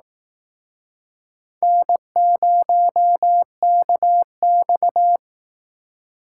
Morse code, heard as a keyed tone, about 19 wpm, 700 Hz. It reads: N0KX